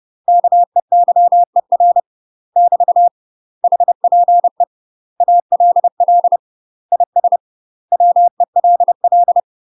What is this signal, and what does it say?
Morse code, heard as a keyed tone, about 30 wpm, 700 Hz. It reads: KEYER <BT> HPE ALL IS WELL
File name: UK MASTER QSOs TRAINING / 30WPM / UK MASTER QSOs 0030